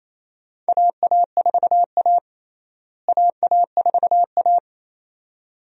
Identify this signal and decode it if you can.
Morse code, heard as a keyed tone, about 28 wpm, 700 Hz. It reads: AA4A AA4A
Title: USA Call Signs - 4 Characters - Repeated 2x - 28wpm